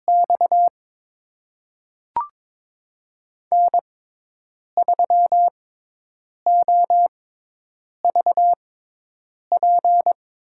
Morse code, heard as a keyed tone, about 22 wpm, spaced as for 8 wpm, 700 Hz. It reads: X N3OVP
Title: USA Call Signs - 5 Characters - 6x Character Spacing - 22wpm